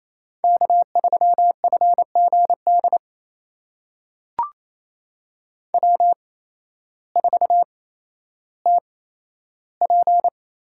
Morse code, heard as a keyed tone, about 28 wpm, spaced as for 13 wpm, 700 Hz. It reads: K3FGD W4TP
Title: USA Call Signs - 5 Characters - 8x Character Spacing - 28wpm